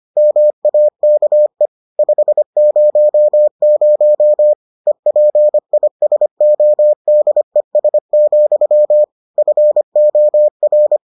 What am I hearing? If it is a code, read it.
Morse code, heard as a keyed tone, about 25 wpm, 600 Hz. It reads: MAKE 500 EPISODES, FOR